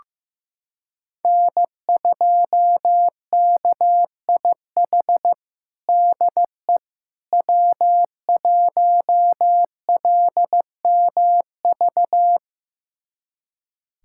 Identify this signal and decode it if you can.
Morse code, heard as a keyed tone, about 15 wpm, 700 Hz. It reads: N2KIH DE W1LMV
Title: USA Callsign DE Callsign - 5 Characters - 15wpm